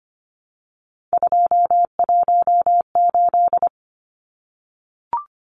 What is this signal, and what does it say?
Morse code, heard as a keyed tone, about 25 wpm, 700 Hz. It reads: 218
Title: Numbers - 3-Digit 25wpm